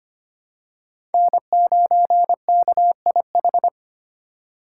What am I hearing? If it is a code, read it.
Morse code, heard as a keyed tone, about 25 wpm, 700 Hz. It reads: N9KIH
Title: USA Call Signs - 3x Character Spacing - 25wpm